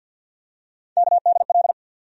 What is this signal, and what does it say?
Morse code, heard as a keyed tone, about 51 wpm, 700 Hz. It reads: KDL
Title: USA Call Sign Postfixes - 3 Letters 50wpm